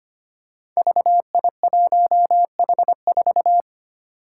Morse code, heard as a keyed tone, about 25 wpm, 700 Hz. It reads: VI1H4
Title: Five Challenging Letter-Number Combinations 25wpm